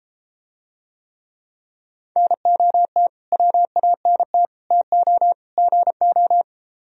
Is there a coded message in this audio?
Yes: Morse code from 2.2 to 6.4 s, heard as a keyed tone, about 33 wpm, 700 Hz. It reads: NOT WANT TO GO